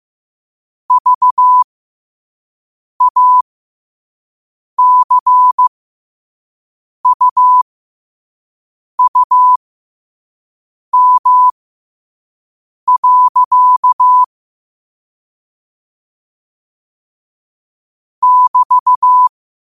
Morse code, heard as a keyed tone, about 15 wpm, spaced as for 5 wpm, 1000 Hz. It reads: VACUUM. <BT>
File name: qotdfiles / 5wpm